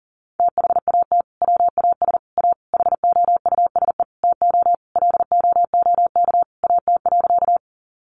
Morse code, heard as a keyed tone, about 40 wpm, 700 Hz. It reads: THAT WAS A HOUSE TO LOOK AT.
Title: Sentences from Top 200 Words 40wpm